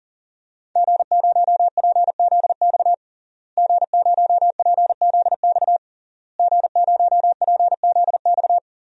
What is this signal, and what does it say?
Morse code, heard as a keyed tone, about 40 wpm, 700 Hz. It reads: G0PZX G0PZX G0PZX
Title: DXCC-100 Call Signs - 5 Characters - Repeated 3x - 40wpm